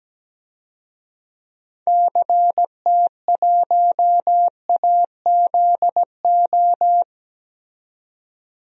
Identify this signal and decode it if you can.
Morse code, heard as a keyed tone, about 17 wpm, 700 Hz. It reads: CT1AZO
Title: DXCC-100 Call Signs - 6 Characters - 17wpm